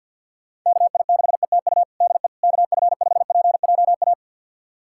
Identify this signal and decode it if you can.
Morse code, heard as a keyed tone, about 52 wpm, 700 Hz. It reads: KI6ETU DE KF5PJA
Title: USA Callsign DE Callsign - 6 Characters - 50wpm